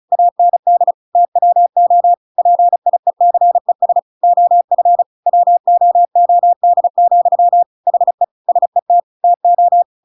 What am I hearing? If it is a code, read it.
Morse code, heard as a keyed tone, about 36 wpm, 700 Hz. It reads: AND TWO PIECES OF WOOD, HE SET TO
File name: Pinocchio / Pinocchio ch8 35 35 ik1hge